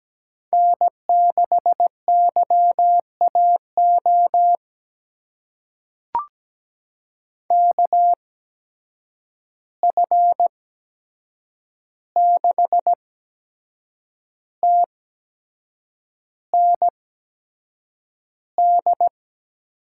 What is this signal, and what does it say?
Morse code, heard as a keyed tone, about 17 wpm, spaced as for 7 wpm, 700 Hz. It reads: N6YAO KF6TND
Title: USA Call Signs - 8x Character Spacing - 17wpm